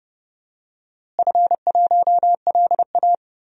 Morse code, heard as a keyed tone, about 30 wpm, 700 Hz. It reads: F1LA
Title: DXCC-100 Call Signs - 4 Characters - Repeated 3x - 30wpm